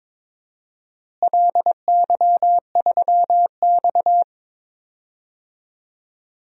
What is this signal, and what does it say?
Morse code, heard as a keyed tone, about 22 wpm, 700 Hz. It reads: LY3X
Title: DXCC-100 Call Signs - 4 Characters - Repeated 3x - 22wpm